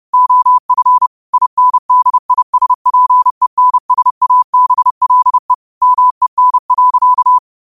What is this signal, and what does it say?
Morse code, heard as a keyed tone, about 30 wpm, 1000 Hz. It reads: OF INDISPENSABLE MEN.
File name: qotdfiles / 30wpm